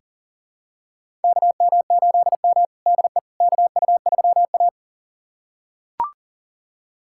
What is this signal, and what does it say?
Morse code, heard as a keyed tone, about 40 wpm, 700 Hz. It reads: KM8M DE KU3A
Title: USA Callsign DE Callsign - 4 Characters - 40wpm